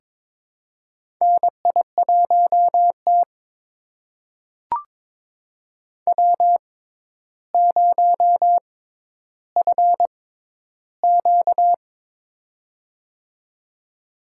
Morse code, heard as a keyed tone, about 22 wpm, spaced as for 11 wpm, 700 Hz. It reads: NI1T W0FQ
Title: USA Call Signs - 4 Characters - 6x Character Spacing - 22wpm